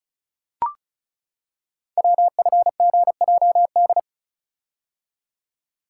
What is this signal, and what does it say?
Morse code, heard as a keyed tone, about 36 wpm, 700 Hz. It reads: WFGJD
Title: Code Groups - 5 Letter:Numbers - 35wpm